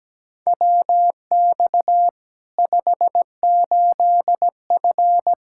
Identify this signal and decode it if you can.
Morse code, heard as a keyed tone, about 17 wpm, 700 Hz. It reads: WX 58F
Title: Practice QSO - Exchange 2 - Info - 17wpm